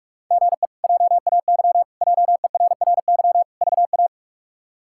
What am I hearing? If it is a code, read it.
Morse code, heard as a keyed tone, about 46 wpm, 700 Hz. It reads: GE JAY JERRY VA
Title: K1USN SST QSO Practice - 45wpm